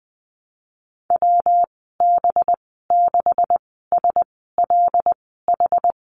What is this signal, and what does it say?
Morse code, heard as a keyed tone, about 20 wpm, spaced as for 15 wpm, 700 Hz. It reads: WB6SLH
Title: 186 Lesson 30g - Letter b - Call Signs 20wpm